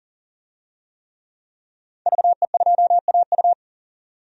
Morse code, heard as a keyed tone, about 40 wpm, 700 Hz. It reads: VE2AU
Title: DXCC-100 Call Signs - 5 Characters - 40wpm